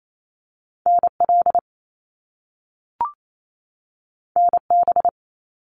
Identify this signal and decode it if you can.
Morse code, heard as a keyed tone, about 28 wpm, 700 Hz. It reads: NL NB
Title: Canadian Provinces and Territories 28wpm